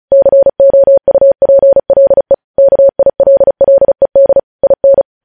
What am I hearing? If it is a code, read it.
Morse code, heard as a keyed tone, about 36 wpm, 550 Hz. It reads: COUPLE KILLED IN
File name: podcast / News headlines-1x-35x35